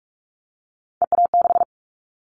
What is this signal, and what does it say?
Morse code, heard as a keyed tone, about 44 wpm, 700 Hz. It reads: EA6
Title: Callsign Prefixes for Top 100 DXCC - 45wpm